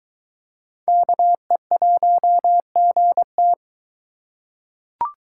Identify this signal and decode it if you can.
Morse code, heard as a keyed tone, about 22 wpm, 700 Hz. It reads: KE1GT